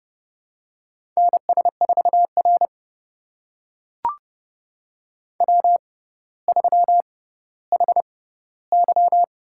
Morse code, heard as a keyed tone, about 30 wpm, spaced as for 15 wpm, 700 Hz. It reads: NS4R W3HY
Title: USA Call Signs - 4 Characters - 6x Character Spacing - 30wpm